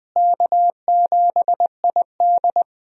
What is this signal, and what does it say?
Morse code, heard as a keyed tone, about 20 wpm, 700 Hz. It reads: K7ID
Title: USA Call Signs - 4 Characters - Repeated 2x - 20wpm